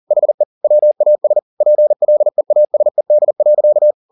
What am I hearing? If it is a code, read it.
Morse code, heard as a keyed tone, about 40 wpm, 600 Hz. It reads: HE WAS PLEASED.